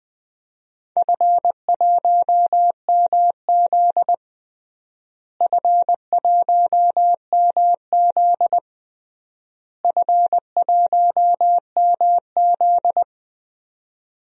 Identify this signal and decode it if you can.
Morse code, heard as a keyed tone, about 19 wpm, 690 Hz. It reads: F1MZ F1MZ F1MZ
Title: DXCC-100 Call Signs - 4 Characters - Repeated 3x - 20wpm